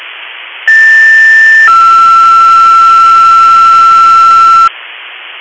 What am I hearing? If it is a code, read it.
Two-tone page, heard as a one-second tone, then a three-second tone, about 1800 then 1340 Hz. It is heard from 0.7 to 4.7 s.